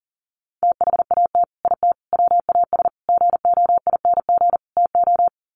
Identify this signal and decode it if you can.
Morse code, heard as a keyed tone, about 40 wpm, 700 Hz. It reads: THAT IT WAS GOING TO
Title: Sets of 5 Words Encore2 40wpm